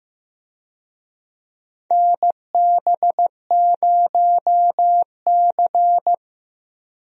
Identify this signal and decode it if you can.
Morse code, heard as a keyed tone, about 15 wpm, 700 Hz. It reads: NB0C
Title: USA Call Signs - 4 Characters - Repeated 3x - 15wpm